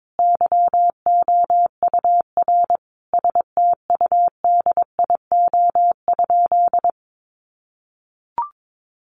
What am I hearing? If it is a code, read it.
Morse code, heard as a keyed tone, about 22 wpm, 700 Hz. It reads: YOUR STUDIO?
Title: Sentences from Top 1200 Words 22wpm